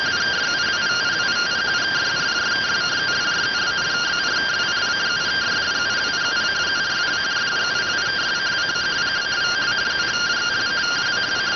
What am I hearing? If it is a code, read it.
Two-tone teleprinter signal, about 170 Hz shift.